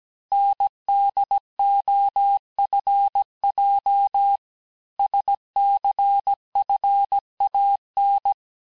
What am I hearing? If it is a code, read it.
Morse code, heard as a keyed tone, about 17 wpm, 800 Hz. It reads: NDOFJ SCFAN
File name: corrige-cours-1581-texte-chiffre-vitesse-16-mpm